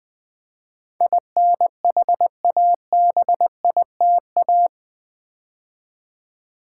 Morse code, heard as a keyed tone, about 20 wpm, 700 Hz. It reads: INHABITA
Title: Build A Word - 4 Letters At A Time - 10-Letter Words - 20wpm